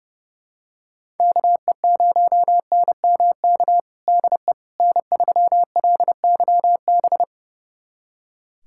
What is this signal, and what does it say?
Morse code, heard as a keyed tone, about 30 wpm, 700 Hz. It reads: KE0NMK DE N3LYB
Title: USA Callsign DE Callsign 30wpm